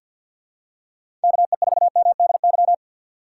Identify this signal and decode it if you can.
Morse code, heard as a keyed tone, about 52 wpm, 700 Hz. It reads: KE4MDY
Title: USA Call Signs - 6 Characters - 3x Character Spacing - 50wpm